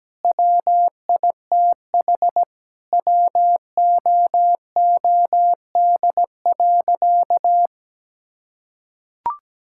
Morse code, heard as a keyed tone, about 17 wpm, 700 Hz. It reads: WITH WOOD.
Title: Sentences from Top 1000 Words - Review Part 6 - 17wpm